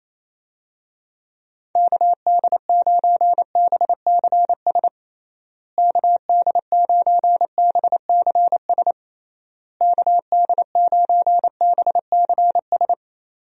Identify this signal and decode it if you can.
Morse code, heard as a keyed tone, about 27 wpm, 700 Hz. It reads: KD9BCS KD9BCS KD9BCS
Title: USA Call Signs - 6 Characters - Repeated 3x - 28wpm